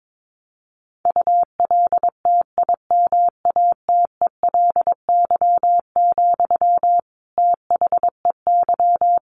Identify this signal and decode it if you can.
Morse code, heard as a keyed tone, about 22 wpm, 700 Hz. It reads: ULTIMATELY, THEY
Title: Sentences from Top 1200 Words 22wpm